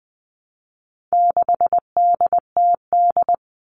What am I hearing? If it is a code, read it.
Morse code, heard as a keyed tone, about 20 wpm, 700 Hz. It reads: 6DTD
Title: All Cut Numbers - 4-Digit 20wpm